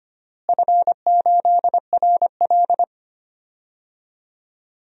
Morse code, heard as a keyed tone, about 25 wpm, 700 Hz. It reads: F8RL
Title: DXCC-100 Call Signs - 4 Characters - Repeated 2x - 25wpm